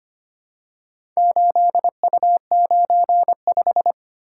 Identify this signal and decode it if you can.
Morse code, heard as a keyed tone, about 25 wpm, 690 Hz. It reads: 8U95
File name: practice / 25wpm / Four Letter-Number Combinations - 25wpm